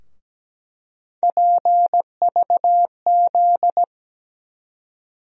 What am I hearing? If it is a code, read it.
Morse code, heard as a keyed tone, about 17 wpm, 700 Hz. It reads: PVZ
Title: USA Call Sign Postfixes - 3 Letters 17wpm